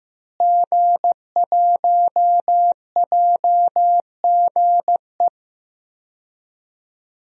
Morse code, heard as a keyed tone, about 15 wpm, 700 Hz. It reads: G1JGE
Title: DXCC-100 Call Signs - 5 Characters - Repeated 3x - 15wpm